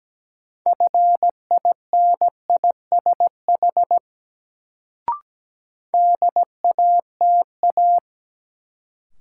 Morse code, heard as a keyed tone, about 17 wpm, 700 Hz. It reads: FINISH DATA
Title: Sentences from Top 1000 Words - Review Part 3 - New Word Drill - 17wpm